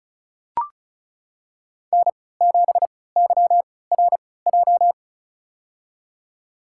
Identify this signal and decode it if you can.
Morse code, heard as a keyed tone, about 36 wpm, spaced as for 20 wpm, 700 Hz. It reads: N7YRJ
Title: USA Call Signs - 5 Characters - 3x Character Spacing - 35wpm